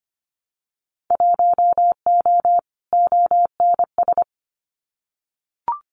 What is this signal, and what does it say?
Morse code, heard as a keyed tone, about 25 wpm, 700 Hz. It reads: 1O ONS
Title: Winter Field Day Exchanges Encore 25wpm